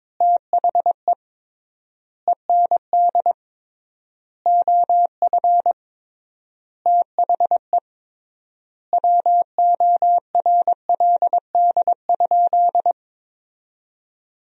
Morse code, heard as a keyed tone, about 22 wpm, 700 Hz. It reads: THE END OF THE WORLD?
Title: 3x Word-Spacing - Sentences from Top 200 Words - 22wpm